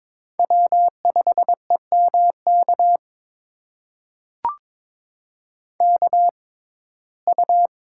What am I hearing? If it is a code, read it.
Morse code, heard as a keyed tone, about 23 wpm, spaced as for 16 wpm, 700 Hz. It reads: W5EMK KU
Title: USA Call Signs - 5 Characters - 6x Character Spacing - 22wpm